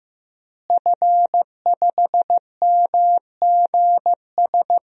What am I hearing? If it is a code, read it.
Morse code, heard as a keyed tone, about 15 wpm, 700 Hz. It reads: F5MGS
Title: DXCC-100 Call Signs - 5 Characters - Repeated 3x - 15wpm